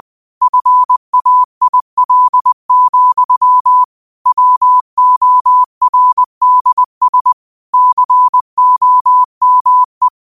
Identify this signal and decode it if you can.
Morse code, heard as a keyed tone, about 20 wpm, 1000 Hz. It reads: FAIL, WORDS COME